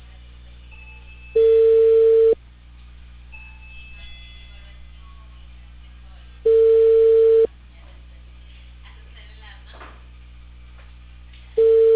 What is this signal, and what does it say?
Ringback tone (China, 450 Hz), with speech.